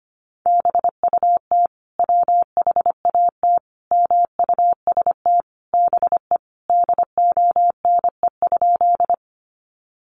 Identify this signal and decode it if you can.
Morse code, heard as a keyed tone, about 25 wpm, 700 Hz. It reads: BUT WHAT MUST BE DONE?
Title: Sentences from Top 500 Words 25wpm